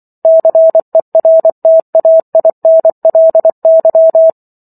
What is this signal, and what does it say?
Morse code, heard as a keyed tone, about 24 wpm, 650 Hz. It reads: CERTAINLY